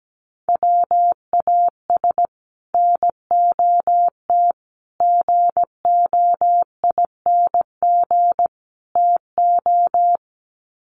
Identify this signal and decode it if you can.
Morse code, heard as a keyed tone, about 17 wpm, 700 Hz. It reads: WAS NOT GOING TO
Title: Sets of 4 Words 17wpm